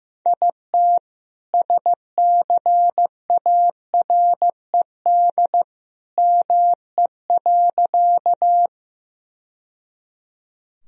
Morse code, heard as a keyed tone, about 15 wpm, 700 Hz. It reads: IT SCARED ME.
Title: Sentences from Top 1000 Words - Review Part 4 - 15wpm